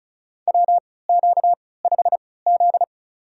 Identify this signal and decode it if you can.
Morse code, heard as a keyed tone, about 36 wpm, spaced as for 20 wpm, 700 Hz. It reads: WQ5Z
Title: USA Call Signs - 4 Characters - 3x Character Spacing - 35wpm